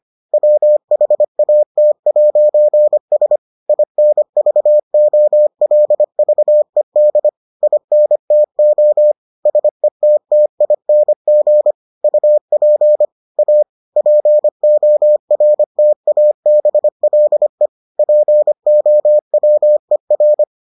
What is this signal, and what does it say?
Morse code, heard as a keyed tone, about 25 wpm, 600 Hz. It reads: WHAT'S INVOLVED INTO SETTING UP A PORTABLE POWER